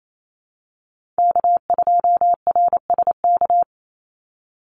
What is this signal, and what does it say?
Morse code, heard as a keyed tone, about 28 wpm, 700 Hz. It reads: K2RSK
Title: Call Signs - 28wpm